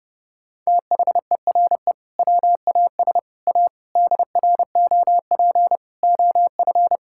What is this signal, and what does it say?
Morse code, heard as a keyed tone, about 32 wpm, 700 Hz. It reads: THERE WAS A DROP OF
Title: Sentences from Top 1600 Words 30wpm